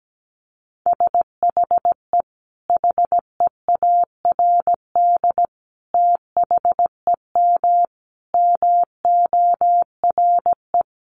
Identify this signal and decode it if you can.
Morse code, heard as a keyed tone, about 17 wpm, 700 Hz. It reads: SHE HEARD THEM MORE
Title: Sentences from Top 1300 Words 17wpm